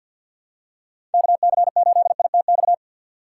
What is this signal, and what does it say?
Morse code, heard as a keyed tone, about 53 wpm, 690 Hz. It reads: KC8ITX
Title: USA Call Signs - 2x Character Spacing - 50wpm